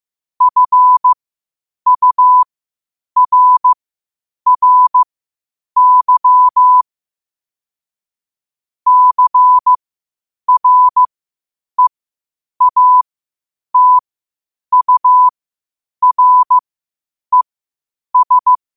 Morse code, heard as a keyed tone, about 15 wpm, spaced as for 8 wpm, 1000 Hz. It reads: FURRY CREATURES
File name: qotdfiles / 8wpm